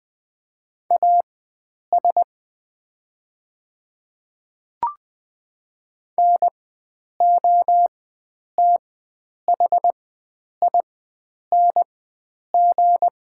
Morse code, heard as a keyed tone, about 20 wpm, spaced as for 9 wpm, 690 Hz. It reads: AS NOTHING
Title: 4x Character Spacing - Top 500 Words - 20wpm